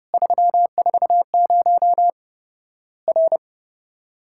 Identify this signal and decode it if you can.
Morse code, heard as a keyed tone, about 30 wpm, 700 Hz. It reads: 340 R
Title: Worked All Europe DX Contest - QTC Exchange - 30wpm